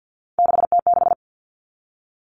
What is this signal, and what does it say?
Morse code, heard as a keyed tone, about 50 wpm, 700 Hz. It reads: 6T6
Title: All Cut Numbers - 3-Digit 50wpm